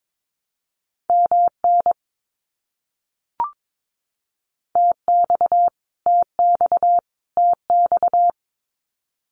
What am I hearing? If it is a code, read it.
Morse code, heard as a keyed tone, about 22 wpm, 700 Hz. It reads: MN TX TX TX
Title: US States Repeated 3x 22wpm